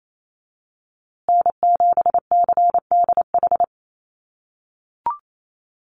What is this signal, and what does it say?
Morse code, heard as a keyed tone, about 28 wpm, 700 Hz. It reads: N7CDH
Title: USA Call Signs Repeated 3x 28wpm